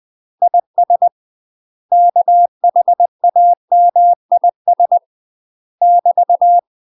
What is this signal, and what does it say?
Morse code, heard as a keyed tone, about 20 wpm, 700 Hz. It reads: IS KHAMIS <BT>
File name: UK MASTER QSOs TRAINING/20WPM